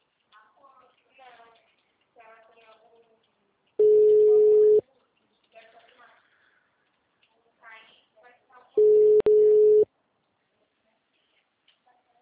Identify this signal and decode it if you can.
Ringback tone (Europe, 425 Hz), with speech.